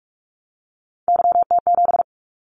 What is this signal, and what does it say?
Morse code, heard as a keyed tone, about 46 wpm, 700 Hz. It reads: YT7
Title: Callsign Prefixes for Top 25 DXCC - 45wpm